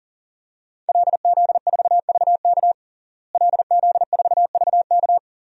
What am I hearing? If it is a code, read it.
Morse code, heard as a keyed tone, about 40 wpm, 700 Hz. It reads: LZ4VK LZ4VK
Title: DXCC-100 Call Signs - 5 Characters - Repeated 3x - 40wpm